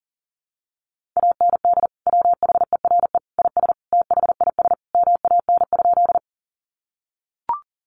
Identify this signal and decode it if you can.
Morse code, heard as a keyed tone, about 40 wpm, 700 Hz. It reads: AND WHERE IS THIS MAN?
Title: Sentences from Top 200 Words 40wpm